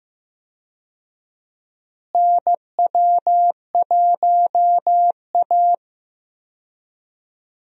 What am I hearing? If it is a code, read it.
Morse code, heard as a keyed tone, about 15 wpm, 700 Hz. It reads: NW1A
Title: USA Call Signs - 4 Characters - Repeated 3x - 15wpm